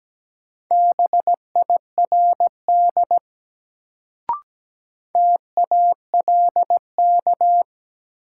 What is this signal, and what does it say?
Morse code, heard as a keyed tone, about 17 wpm, 700 Hz. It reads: BIRD TALK
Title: Sentences from Top 400 Words - New Word Drill - 17wpm